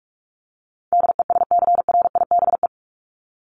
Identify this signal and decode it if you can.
Morse code, heard as a keyed tone, about 44 wpm, 700 Hz. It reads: DESCRIBE
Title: ED Suffix - Top 100 Words with Base Words 45wpm